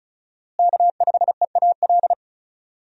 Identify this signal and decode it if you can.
Morse code, heard as a keyed tone, about 34 wpm, 700 Hz. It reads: K5EAL
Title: USA Call Signs - 5 Characters - Repeated 3x - 35wpm